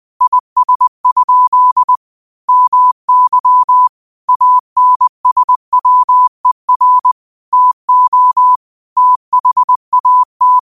Morse code, heard as a keyed tone, about 20 wpm, 1000 Hz. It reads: IS? MY ANSWER TO THAT
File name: qotdfiles / 20wpm